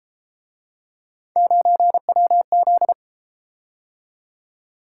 Morse code, heard as a keyed tone, about 32 wpm, 700 Hz. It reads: 9WZ